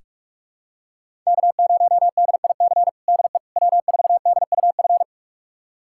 Morse code, heard as a keyed tone, about 46 wpm, 700 Hz. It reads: K0DIC DE W4DUF
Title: USA Callsign DE Callsign - 5 Characters - 45wpm